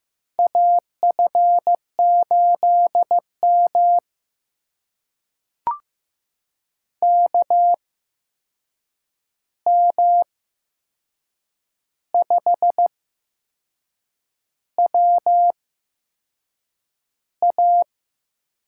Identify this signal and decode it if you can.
Morse code, heard as a keyed tone, about 15 wpm, spaced as for 6 wpm, 700 Hz. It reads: AF8M KM5WA
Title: USA Call Signs - 8x Character Spacing - 15wpm